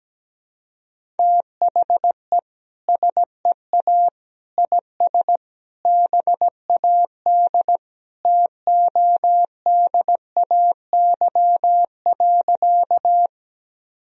Morse code, heard as a keyed tone, about 17 wpm, 700 Hz. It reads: THE SEA IS BAD TODAY.
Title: Sentences from Top 1000 Words - Review Part 1 - 17wpm